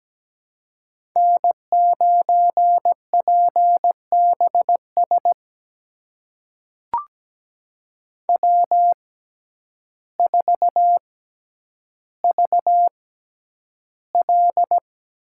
Morse code, heard as a keyed tone, about 17 wpm, spaced as for 9 wpm, 700 Hz. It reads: N9PBS W4VL
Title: USA Call Signs - 5 Characters - 6x Character Spacing - 17wpm